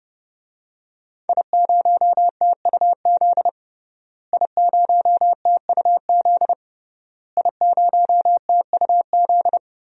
Morse code, heard as a keyed tone, about 32 wpm, 700 Hz. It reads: I0TUZ I0TUZ I0TUZ
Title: DXCC-100 Call Signs - 5 Characters - Repeated 3x - 30wpm